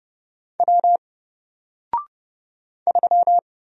Morse code, heard as a keyed tone, about 29 wpm, 700 Hz. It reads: W 3
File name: Single-Letter-Number-30wpm